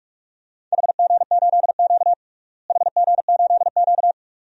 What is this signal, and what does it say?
Morse code, heard as a keyed tone, about 46 wpm, 700 Hz. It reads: HG8Q HG8Q
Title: DXCC-100 Call Signs - 4 Characters - Repeated 2x - 45wpm